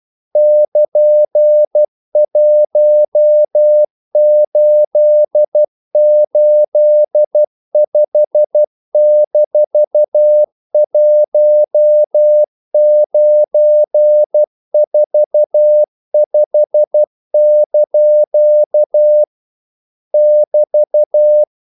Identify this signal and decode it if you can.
Morse code, heard as a keyed tone, about 12 wpm, 600 Hz. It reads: <KN>1885-1945) <BT>
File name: CW / Practice / EnglishQuotations0312